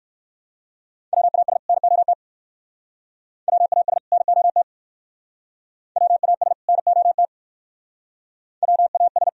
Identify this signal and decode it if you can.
Morse code, heard as a keyed tone, about 67 wpm, 700 Hz. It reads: WAS NOT WAS NOT WAS NOT WAS